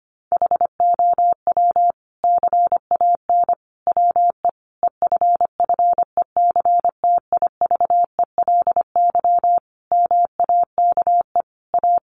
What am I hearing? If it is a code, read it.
Morse code, heard as a keyed tone, about 25 wpm, 700 Hz. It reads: HOW CAN WE EFFECTIVELY MAKE A
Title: Sentences from Top 1400 Words 25wpm